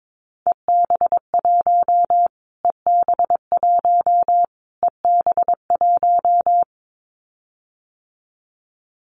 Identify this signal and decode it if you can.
Morse code, heard as a keyed tone, about 22 wpm, 700 Hz. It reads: EB1 EB1 EB1
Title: Callsign Prefixes for Top 25 DXCC - Repeated 3x - 22wpm